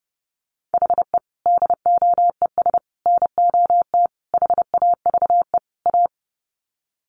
Morse code, heard as a keyed tone, about 30 wpm, 700 Hz. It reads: HE DOES NOT HAVE A
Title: Sets of 5 Words Encore 30wpm